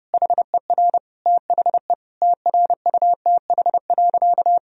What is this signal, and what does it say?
Morse code, heard as a keyed tone, about 30 wpm, 700 Hz. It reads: HER THE TRUTH.
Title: Sentences from Top 1000 Words - Review Part 4 - 30wpm